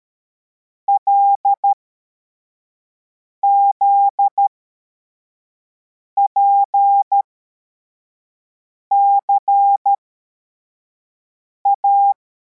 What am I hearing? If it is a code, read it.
Morse code, heard as a keyed tone, about 13 wpm, spaced as for 4 wpm, 800 Hz. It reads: LZPCA